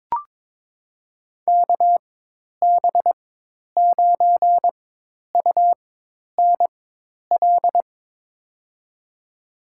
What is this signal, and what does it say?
Morse code, heard as a keyed tone, about 22 wpm, spaced as for 10 wpm, 700 Hz. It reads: KB9UNL
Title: USA Call Signs - 4x Character Spacing - 22wpm